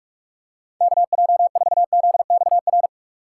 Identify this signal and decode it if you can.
Morse code, heard as a keyed tone, about 46 wpm, 700 Hz. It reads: KJ4ZXR
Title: USA Call Signs - 6 Characters - 6x Character Spacing - 45wpm